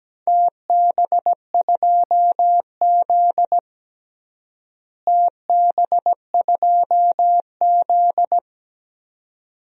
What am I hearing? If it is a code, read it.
Morse code, heard as a keyed tone, about 17 wpm, 700 Hz. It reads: TB2Z TB2Z
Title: DXCC-100 Call Signs - 4 Characters - Repeated 2x - 17wpm